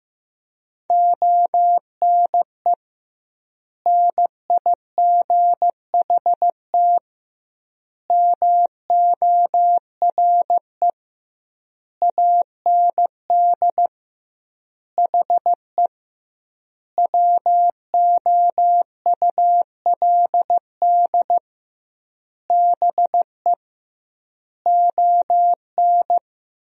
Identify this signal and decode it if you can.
Morse code, heard as a keyed tone, about 15 wpm, 700 Hz. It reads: ONE NIGHT MORE AND HE WOULD BE ON
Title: 2x Word-Spacing - Sentences from Top 300 Words - 15wpm